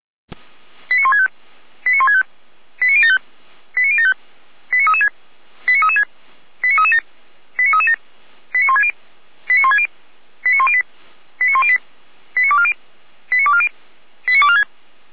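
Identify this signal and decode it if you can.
Five-tone selective call (ZVEI), heard as quick runs of five tones, tones of about 80 ms. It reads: 89165 89165 89075 89075 89336 89336 89337 89337 89160 89160 89108 89108 89240 89240 89246
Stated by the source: Five-tone ZVEI selective call received on SDR.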